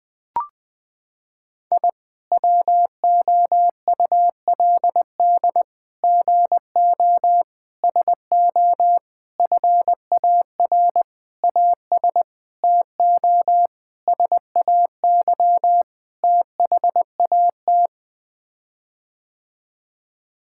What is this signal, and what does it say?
Morse code, heard as a keyed tone, about 20 wpm, 700 Hz. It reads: I WOULD GO SO FAR AS TO SAY THAT
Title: Sets of 9 Words - 20wpm